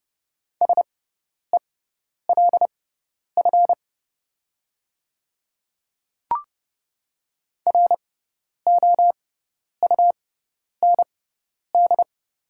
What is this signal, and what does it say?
Morse code, heard as a keyed tone, about 30 wpm, spaced as for 10 wpm, 700 Hz. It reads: SELF ROUND
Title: 6x Character Spacing - Top 200 Words - 30wpm